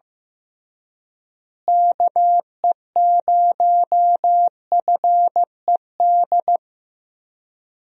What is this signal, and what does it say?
Morse code, heard as a keyed tone, about 15 wpm, 700 Hz. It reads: KE0FED
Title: USA Call Signs - 6 Characters - Repeated 3x - 15wpm